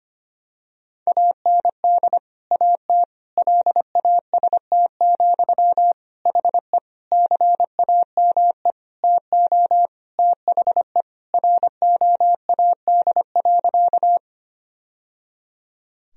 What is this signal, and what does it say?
Morse code, heard as a keyed tone, about 25 wpm, 700 Hz. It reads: AND AT LAST, HE CAME TO THE ROAD.
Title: Sentences from Top 1000 Words - Review Part 2 - 25wpm